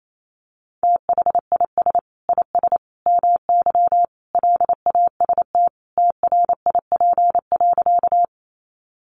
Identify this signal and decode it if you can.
Morse code, heard as a keyed tone, about 28 wpm, 700 Hz. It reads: THIS IS MY LAST TRIP.
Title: Sentences from Top 1300 Words 28wpm